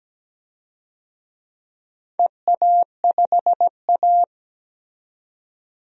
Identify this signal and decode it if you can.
Morse code, heard as a keyed tone, about 17 wpm, 700 Hz. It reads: EA5A
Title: DXCC-100 Call Signs - 4 Characters - 17wpm